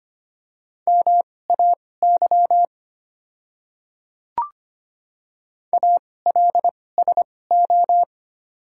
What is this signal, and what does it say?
Morse code, heard as a keyed tone, about 25 wpm, spaced as for 18 wpm, 700 Hz. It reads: MAY ALSO
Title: 2x Character Spacing - Top 200 Words - 25wpm